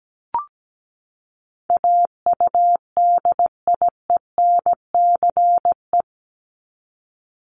Morse code, heard as a keyed tone, about 17 wpm, 700 Hz. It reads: AUDIENCE
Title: Top 751-1000 Words 17wpm